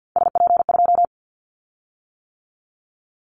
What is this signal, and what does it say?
Morse code, heard as a keyed tone, about 50 wpm, 700 Hz. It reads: SP2
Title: Callsign Prefixes for Top 25 DXCC - 50wpm